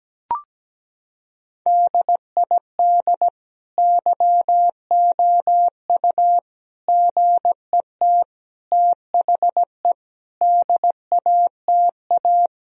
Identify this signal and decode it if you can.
Morse code, heard as a keyed tone, about 17 wpm, 700 Hz. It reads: DID YOU GET THE DATA
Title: Sentences from Top 1000 Words - Review Part 3 - 17wpm